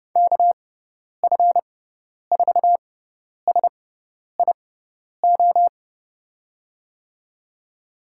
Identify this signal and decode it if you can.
Morse code, heard as a keyed tone, about 30 wpm, spaced as for 10 wpm, 700 Hz. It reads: KF4SIO
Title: USA Call Signs - 6 Characters - 6x Character Spacing - 30wpm